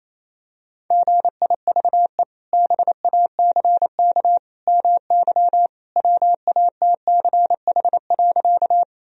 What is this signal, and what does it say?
Morse code, heard as a keyed tone, about 27 wpm, 700 Hz. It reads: GIVE BACK MY WATCH.
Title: Sentences from Top 1000 Words - Review Part 1 - 28wpm